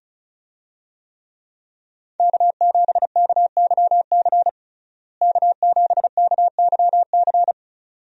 Morse code, heard as a keyed tone, about 34 wpm, 700 Hz. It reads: K7KYC K7KYC
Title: USA Call Signs - 5 Characters - Repeated 2x - 35wpm